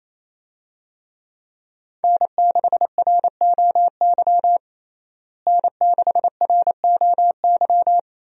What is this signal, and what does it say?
Morse code, heard as a keyed tone, about 28 wpm, 700 Hz. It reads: N6ROY N6ROY
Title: USA Call Signs - 5 Characters - Repeated 3x - 28wpm